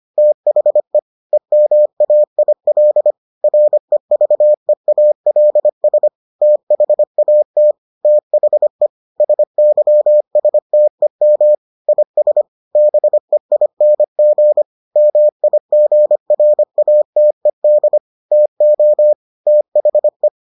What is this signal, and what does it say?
Morse code, heard as a keyed tone, about 25 wpm, 600 Hz. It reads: THE EMAIL REVEALS THAT THE SYSTEM IS BEING MIGRATED TO THE